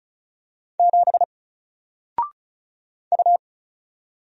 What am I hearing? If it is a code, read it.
Morse code, heard as a keyed tone, about 36 wpm, 700 Hz. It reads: 7 U
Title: Single Character (Letter:Number:Punctuation:Prosign) Alt 35wpm